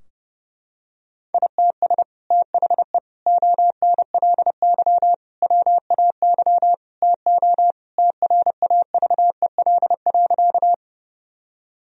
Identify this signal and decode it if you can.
Morse code, heard as a keyed tone, about 30 wpm, 700 Hz. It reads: ITS THE ONLY WAY TO TRAVEL.
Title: Sentences from Top 1000 Words - Review Part 2 - 30wpm